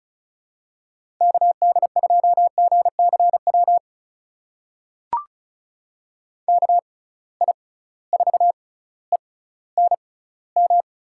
Morse code, heard as a keyed tone, about 36 wpm, spaced as for 18 wpm, 700 Hz. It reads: KD2GCW KI4ENM
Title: USA Call Signs - 6 Characters - 6x Character Spacing - 35wpm